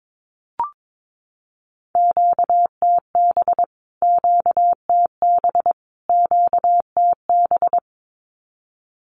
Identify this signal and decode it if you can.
Morse code, heard as a keyed tone, about 22 wpm, 700 Hz. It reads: QTB QTB QTB
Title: Q-Codes Repeated 3x 22wpm